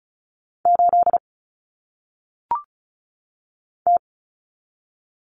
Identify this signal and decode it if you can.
Morse code, heard as a keyed tone, about 36 wpm, 700 Hz. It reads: 8 T
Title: Common Cut Numbers - 1-Digit 35wpm